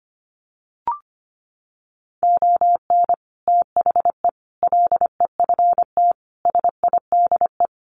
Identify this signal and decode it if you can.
Morse code, heard as a keyed tone, about 25 wpm, 700 Hz. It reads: ON THE LEFT SIDE
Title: Sets of 5 Words Encore2 25wpm